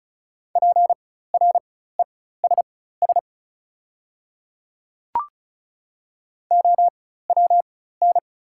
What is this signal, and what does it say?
Morse code, heard as a keyed tone, about 34 wpm, spaced as for 16 wpm, 700 Hz. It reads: PRESS OWN
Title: 4x Character Spacing - Top 300 Words - 35wpm